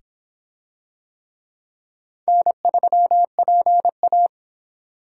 Morse code, heard as a keyed tone, about 23 wpm, 700 Hz. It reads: N3PA